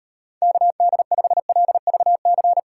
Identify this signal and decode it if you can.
Morse code, heard as a keyed tone, about 38 wpm, 700 Hz. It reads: KD5LVC